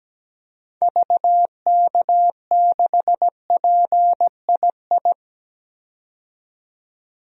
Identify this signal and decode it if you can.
Morse code, heard as a keyed tone, about 17 wpm, 700 Hz. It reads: VK6PII
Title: DXCC-100 Call Signs - 6 Characters - 17wpm